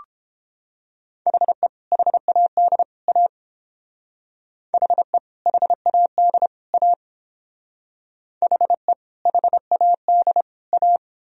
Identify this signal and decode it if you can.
Morse code, heard as a keyed tone, about 30 wpm, 700 Hz. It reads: HE HAD A HE HAD A HE HAD A